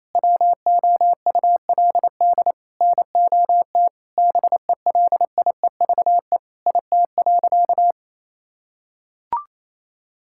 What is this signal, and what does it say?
Morse code, heard as a keyed tone, about 28 wpm, 700 Hz. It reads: WOULD NOT BELIEVE IT.
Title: Sentences from Top 1000 Words - Review Part 6 - 28wpm